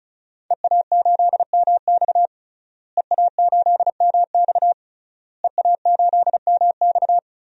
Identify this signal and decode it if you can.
Morse code, heard as a keyed tone, about 34 wpm, 700 Hz. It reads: EA8MX EA8MX EA8MX
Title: DXCC-100 Call Signs - 5 Characters - Repeated 3x - 35wpm